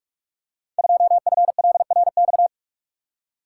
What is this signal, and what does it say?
Morse code, heard as a keyed tone, about 46 wpm, 700 Hz. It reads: 2FLRX
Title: Code Groups - 5 Letter:Numbers - 45wpm